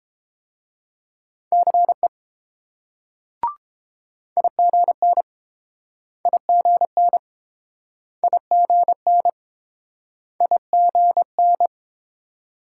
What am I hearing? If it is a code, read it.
Morse code, heard as a keyed tone, about 30 wpm, 700 Hz. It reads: CE IGN IGN IGN IGN